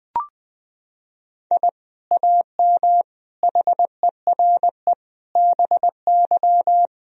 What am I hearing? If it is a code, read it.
Morse code, heard as a keyed tone, about 20 wpm, 700 Hz. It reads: I AM HERE BY
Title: Sentences from Top 1000 Words - Review Part 3 - 20wpm